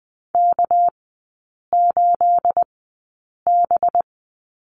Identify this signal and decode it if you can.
Morse code, heard as a keyed tone, about 20 wpm, spaced as for 8 wpm, 690 Hz. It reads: K8B
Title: 212 Lesson 34e - Number 8 - Three Characters 20wpm